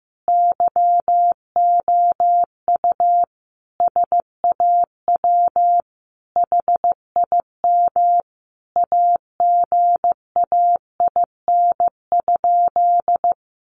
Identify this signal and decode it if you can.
Morse code, heard as a keyed tone, about 15 wpm, 700 Hz. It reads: YOU SAW HIM AGAIN?
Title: Sentences from Top 1400 Words 15wpm